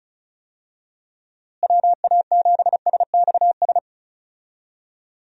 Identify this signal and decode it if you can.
Morse code, heard as a keyed tone, about 34 wpm, 700 Hz. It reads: WA7SXS
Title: USA Call Signs - 6 Characters - Repeated 3x - 35wpm